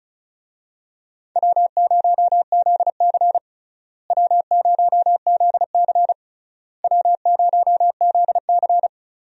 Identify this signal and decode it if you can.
Morse code, heard as a keyed tone, about 36 wpm, 700 Hz. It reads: W0ZC W0ZC W0ZC
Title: USA Call Signs - 4 Characters - Repeated 3x - 35wpm